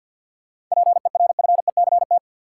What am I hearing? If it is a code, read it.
Morse code, heard as a keyed tone, about 53 wpm, 700 Hz. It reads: PERFECT
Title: Seven Letter Words - 50wpm